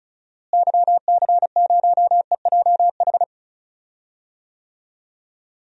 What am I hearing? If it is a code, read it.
Morse code, heard as a keyed tone, about 34 wpm, 700 Hz. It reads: YC0EJH
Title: DXCC-100 Call Signs - 6 Characters - Repeated 3x - 35wpm